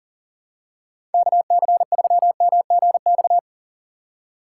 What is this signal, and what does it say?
Morse code, heard as a keyed tone, about 39 wpm, 690 Hz. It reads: KC3MGX
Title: USA Call Signs - 4x Character Spacing - 40wpm